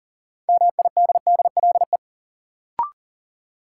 Morse code, heard as a keyed tone, about 40 wpm, 700 Hz. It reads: MIDDLE
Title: Sentences from Top 900 Words - New Word Drill - 40wpm